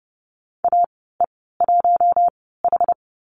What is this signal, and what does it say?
Morse code, heard as a keyed tone, about 30 wpm, spaced as for 17 wpm, 700 Hz. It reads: AE1H
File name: Lesson 14g - Number 2 - Call Signs 30wpm